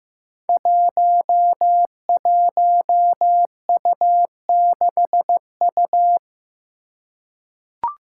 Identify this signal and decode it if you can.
Morse code, heard as a keyed tone, about 15 wpm, 700 Hz. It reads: 11U6U
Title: Five Challenging Letter-Number Combinations 15wpm